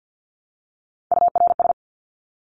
Morse code, heard as a keyed tone, about 50 wpm, 700 Hz. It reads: URS
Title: Abbreviations Repeated 3x - 50wpm